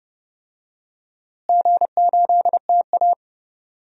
Morse code, heard as a keyed tone, about 31 wpm, 700 Hz. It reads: G8TA
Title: DXCC-100 Call Signs - 4 Characters - Repeated 3x - 30wpm